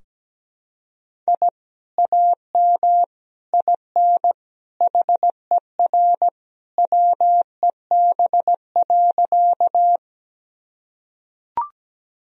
Morse code, heard as a keyed tone, about 17 wpm, 700 Hz. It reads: I AM IN HER WEB.
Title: Sentences from Top 1000 Words - Review Part 4 - 17wpm